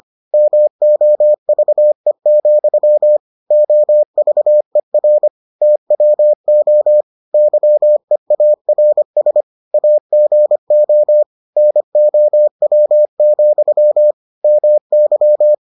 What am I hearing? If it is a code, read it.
Morse code, heard as a keyed tone, about 25 wpm, 600 Hz. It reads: MOVE, OVER TWO YEARS AGO NOW, MY